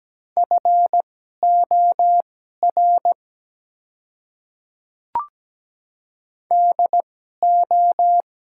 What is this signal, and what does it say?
Morse code, heard as a keyed tone, about 17 wpm, spaced as for 12 wpm, 700 Hz. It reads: FOR DO
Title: 2x Character Spacing - Top 200 Words - 17wpm